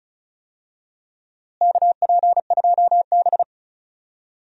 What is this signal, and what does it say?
Morse code, heard as a keyed tone, about 36 wpm, 700 Hz. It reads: KP2B
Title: DXCC-100 Call Signs - 4 Characters - Repeated 3x - 35wpm